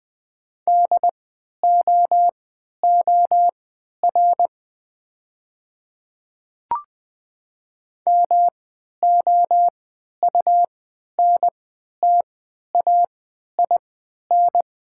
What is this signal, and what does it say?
Morse code, heard as a keyed tone, about 20 wpm, spaced as for 11 wpm, 690 Hz. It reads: DOOR MOUNTAIN
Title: 3x Character Spacing - Top 300 Words - 20wpm